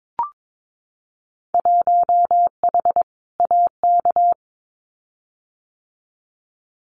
Morse code, heard as a keyed tone, about 22 wpm, 700 Hz. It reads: 1H AK
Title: Winter Field Day Exchanges Encore 22wpm